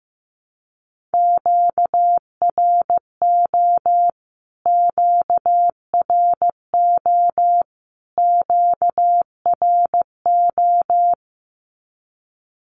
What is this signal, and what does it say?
Morse code, heard as a keyed tone, about 15 wpm, 700 Hz. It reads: QRO QRO QRO
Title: Q-Codes Repeated 3x 15wpm